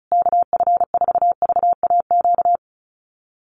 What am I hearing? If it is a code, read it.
Morse code, heard as a keyed tone, about 34 wpm, 700 Hz. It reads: KF4VAQ
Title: USA Call Signs Repeated 3x 35wpm